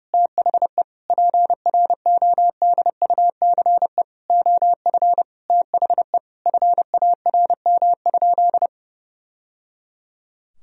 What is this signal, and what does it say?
Morse code, heard as a keyed tone, about 30 wpm, 700 Hz. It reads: THE PRODUCE OF THE FARM?
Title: Sentences from Top 1000 Words - Review Part 3 - 30wpm